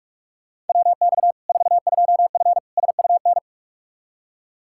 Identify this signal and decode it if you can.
Morse code, heard as a keyed tone, about 45 wpm, 700 Hz. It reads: WX 42F SUN
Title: K1USN SST QSO Practice 45wpm